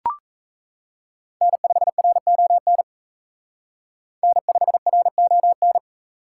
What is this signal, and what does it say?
Morse code, heard as a keyed tone, about 32 wpm, 700 Hz. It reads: N5RON N5RON
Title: Speed Racing - USA 1x3 Call Signs 28wpm